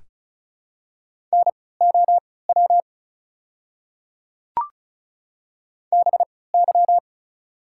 Morse code, heard as a keyed tone, about 36 wpm, spaced as for 20 wpm, 700 Hz. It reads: NOW BY
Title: 3x Character Spacing - Top 100 Words - 35wpm